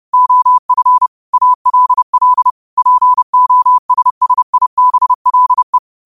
Morse code, heard as a keyed tone, about 30 wpm, 1000 Hz. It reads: OF ALL POSSIBLE
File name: qotdfiles / 30wpm